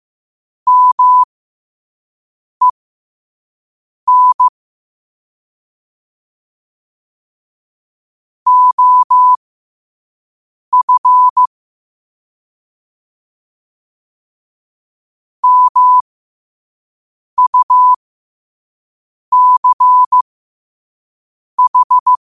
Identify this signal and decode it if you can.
Morse code, heard as a keyed tone, about 15 wpm, spaced as for 5 wpm, 1000 Hz. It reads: MEN OF MUCH
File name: qotdfiles / 5wpm